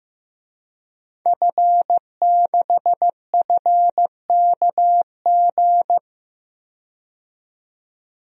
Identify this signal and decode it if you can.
Morse code, heard as a keyed tone, about 15 wpm, 700 Hz. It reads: F6FKG
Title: DXCC-100 Call Signs - 5 Characters - 15wpm